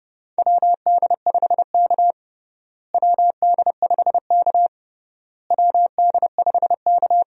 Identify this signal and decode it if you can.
Morse code, heard as a keyed tone, about 32 wpm, 700 Hz. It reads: WD5K WD5K WD5K
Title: USA Call Signs - 4 Characters - Repeated 3x - 30wpm